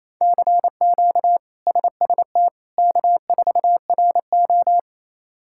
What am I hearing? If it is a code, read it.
Morse code, heard as a keyed tone, about 28 wpm, 700 Hz. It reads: CQ SST K4RO
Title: K1USN SST QSO Practice - 28wpm